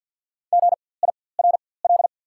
Morse code, heard as a keyed tone, about 48 wpm, spaced as for 23 wpm, 700 Hz. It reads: GIRL
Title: 4x Character Spacing - Top 500 Words - 50wpm